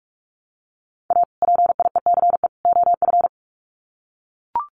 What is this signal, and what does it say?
Morse code, heard as a keyed tone, about 44 wpm, 700 Hz. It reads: A PIECE OF
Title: Sets of 3 Words 45wpm